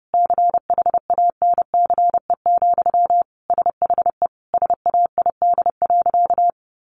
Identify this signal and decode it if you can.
Morse code, heard as a keyed tone, about 30 wpm, 700 Hz. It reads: CHANCE, SHE SAID.
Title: Sentences from Top 800 Words 30wpm